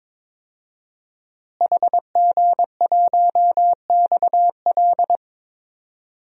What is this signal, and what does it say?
Morse code, heard as a keyed tone, about 22 wpm, 700 Hz. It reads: HG1XL
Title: DXCC-100 Call Signs - 5 Characters - Repeated 2x - 22wpm